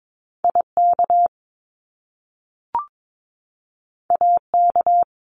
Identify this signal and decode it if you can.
Morse code, heard as a keyed tone, about 22 wpm, 700 Hz. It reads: IK AK
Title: Top Two Letter Combinations in Words 22wpm